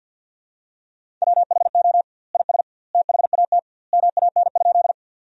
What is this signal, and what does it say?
Morse code, heard as a keyed tone, about 50 wpm, 700 Hz. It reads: WHO IS THAT MAN?
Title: 2x Word-Spacing - Sentences from Top 200 Words - 50wpm